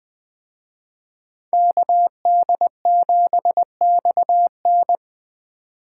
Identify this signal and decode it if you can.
Morse code, heard as a keyed tone, about 21 wpm, 700 Hz. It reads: KD7XN